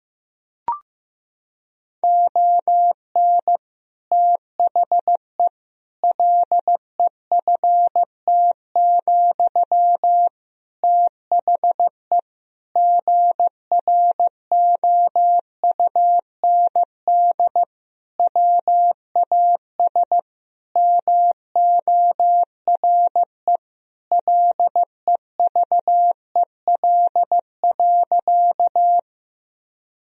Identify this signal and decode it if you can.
Morse code, heard as a keyed tone, about 15 wpm, 700 Hz. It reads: ON THE LEFT, THE GROUND WAS MORE LEVEL.
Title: Sentences from Top 1000 Words - Review Part 4 - 15wpm